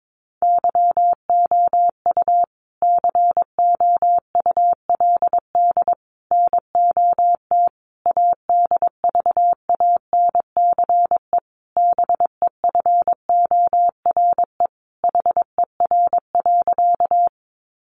Morse code, heard as a keyed tone, about 22 wpm, 700 Hz. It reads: YOU COULD NOT ADVANCE BEFORE HER.
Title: Sentences from Top 1400 Words 22wpm